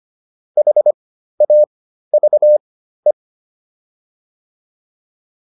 Morse code, heard as a keyed tone, about 25 wpm, spaced as for 13 wpm, 600 Hz. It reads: HAVE